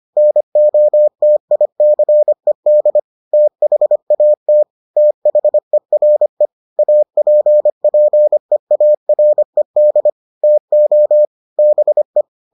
Morse code, heard as a keyed tone, about 25 wpm, 600 Hz. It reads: NOTICED THAT THERE APPEARED TO BE